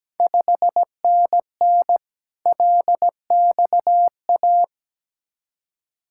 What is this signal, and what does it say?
Morse code, heard as a keyed tone, about 17 wpm, 700 Hz. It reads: 5NN LXA
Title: Ontario QSO Party Exchange 17wpm